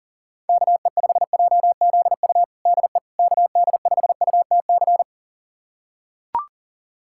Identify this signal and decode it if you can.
Morse code, heard as a keyed tone, about 40 wpm, 700 Hz. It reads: KE5JZU DE KD5UTC
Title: USA Callsign DE Callsign 40wpm